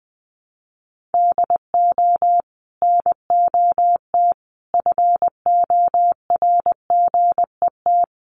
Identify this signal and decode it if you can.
Morse code, heard as a keyed tone, about 20 wpm, 700 Hz. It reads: DO NOT FORGET
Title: Sentences from Top 1200 Words 20wpm